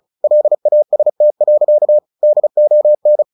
Morse code, heard as a keyed tone, about 34 wpm, 600 Hz. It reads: LAST. DON
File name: the old man and the sea 2 35wpm